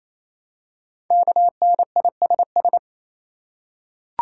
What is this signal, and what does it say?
Morse code, heard as a keyed tone, about 29 wpm, 690 Hz. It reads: KNISS
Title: Code Groups - 5 Letters - 28wpm